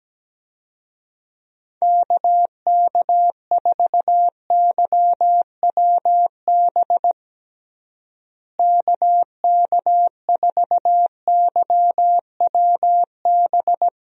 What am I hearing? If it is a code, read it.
Morse code, heard as a keyed tone, about 17 wpm, 700 Hz. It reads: KK4YWB KK4YWB
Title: USA Call Signs - 6 Characters - Repeated 3x - 17wpm